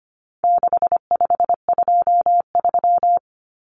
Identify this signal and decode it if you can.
Morse code, heard as a keyed tone, about 25 wpm, 700 Hz. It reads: 6523
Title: Numbers - 4-Digit 25wpm